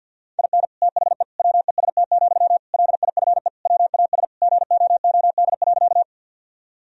Morse code, heard as a keyed tone, about 50 wpm, 700 Hz. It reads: IN THE WEST, LIFE WAS GOOD.
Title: Sentences from Top 1000 Words - Review Part 2 - 50wpm